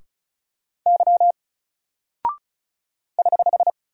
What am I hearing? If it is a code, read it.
Morse code, heard as a keyed tone, about 35 wpm, 700 Hz. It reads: Y <HH>
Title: Single Character (Letter:Number:Punctuation:Prosign) 35wpm